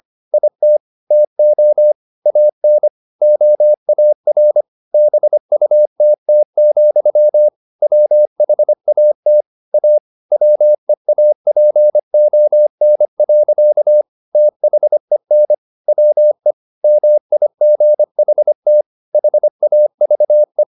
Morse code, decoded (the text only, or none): IT TO AN OAR BUTT, WHAT A WEAPON. THEN WE MIGHT HAVE